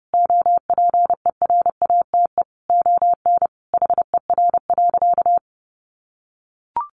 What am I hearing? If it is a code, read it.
Morse code, heard as a keyed tone, about 30 wpm, 700 Hz. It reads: OPERATE ON HER.
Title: Sentences from Top 1500 Words 30wpm